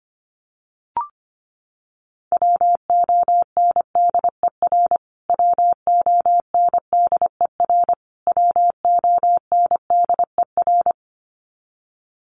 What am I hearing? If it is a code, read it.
Morse code, heard as a keyed tone, about 25 wpm, 700 Hz. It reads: WONDER WONDER WONDER
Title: Top 500 Words Repeated 25wpm